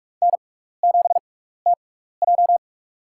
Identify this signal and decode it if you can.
Morse code, heard as a keyed tone, about 42 wpm, spaced as for 15 wpm, 700 Hz. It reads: N7TJ
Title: USA Call Signs - 6x Character Spacing - 45wpm